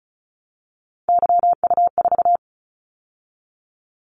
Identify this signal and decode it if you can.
Morse code, heard as a keyed tone, about 36 wpm, 700 Hz. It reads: YU4
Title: Callsign Prefixes for Top 25 DXCC - 35wpm